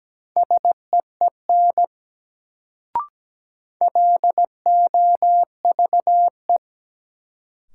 Morse code, heard as a keyed tone, about 17 wpm, 700 Hz. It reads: SEEN LOVE
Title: Sets of 5 Words Encore - New Word Drill - 17wpm